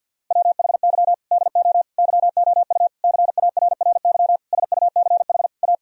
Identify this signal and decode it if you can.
Morse code, heard as a keyed tone, about 50 wpm, 700 Hz. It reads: WHY DO YOU CARRY SUCH A
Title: Sentences from Top 1000 Words - Review Part 1 - 50wpm